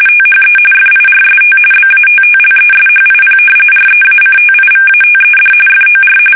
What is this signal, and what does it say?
RTTY, about 45.45 baud. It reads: THE FIRST BATTLESHIP REACTIVATED FOR
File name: fsk850 / battleshipnj